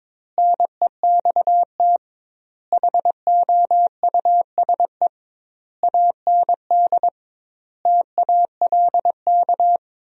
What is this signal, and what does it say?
Morse code, heard as a keyed tone, about 22 wpm, 700 Hz. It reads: NEXT HOUSE AND TALK
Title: 2x Word-Spacing - Sentences from Top 400 Words - 22wpm